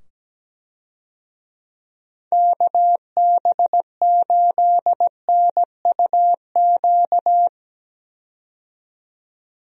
Morse code, heard as a keyed tone, about 17 wpm, 700 Hz. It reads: KB8NUQ
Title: USA Call Signs - 6 Characters - 17wpm